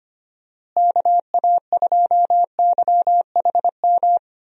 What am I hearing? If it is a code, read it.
Morse code, heard as a keyed tone, about 25 wpm, 700 Hz. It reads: KA2YHM
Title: USA Call Signs - 6 Characters - Repeated 3x - 25wpm